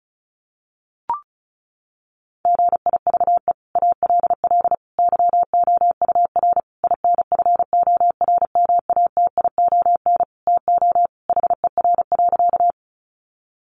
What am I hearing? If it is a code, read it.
Morse code, heard as a keyed tone, about 34 wpm, 700 Hz. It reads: GIVE ALL YOUR INFORMATION TO HER.
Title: Sentences from Top 600 Words 35wpm